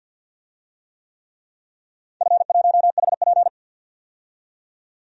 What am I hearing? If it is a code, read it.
Morse code, heard as a keyed tone, about 52 wpm, 700 Hz. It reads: F1HP
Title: DXCC-100 Call Signs - 4 Characters - 50wpm